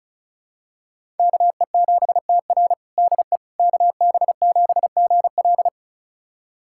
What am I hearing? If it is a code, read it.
Morse code, heard as a keyed tone, about 36 wpm, 700 Hz. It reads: KE7TR DE KB7GL
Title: USA Callsign DE Callsign - 5 Characters - 35wpm